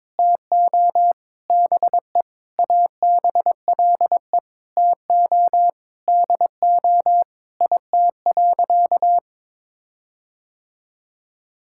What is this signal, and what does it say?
Morse code, heard as a keyed tone, about 21 wpm, 700 Hz. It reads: TO BE ABLE TO DO IT.
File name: practice / 22wpm / Sentences with 4-Word Phrases - 22wpm